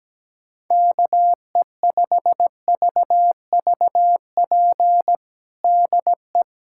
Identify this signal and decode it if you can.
Morse code, heard as a keyed tone, about 17 wpm, 700 Hz. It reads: KE5VVP DE
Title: USA Callsign DE Callsign 17wpm